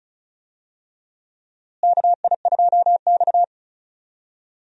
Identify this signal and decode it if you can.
Morse code, heard as a keyed tone, about 35 wpm, 700 Hz. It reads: KI2X
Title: USA Call Signs - 4 Characters - Repeated 3x - 35wpm